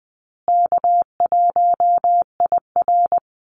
Morse code, heard as a keyed tone, about 20 wpm, 700 Hz. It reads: K1IR
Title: Easy Call Signs Repeated 3x - 20wpm